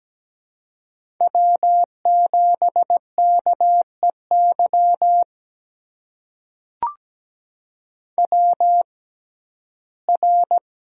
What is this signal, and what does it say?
Morse code, heard as a keyed tone, about 17 wpm, spaced as for 12 wpm, 700 Hz. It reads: W7KEY WR
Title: USA Call Signs - 5 Characters - 6x Character Spacing - 17wpm